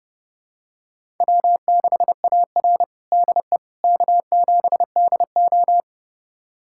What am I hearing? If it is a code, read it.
Morse code, heard as a keyed tone, about 32 wpm, 700 Hz. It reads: W6AR DE K7DO
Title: USA Callsign DE Callsign - 4 Characters - 30wpm